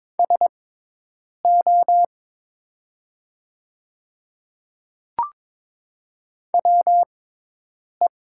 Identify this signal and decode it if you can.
Morse code, heard as a keyed tone, about 22 wpm, spaced as for 8 wpm, 700 Hz. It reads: SO WE
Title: 6x Character Spacing - Top 100 Words - 22wpm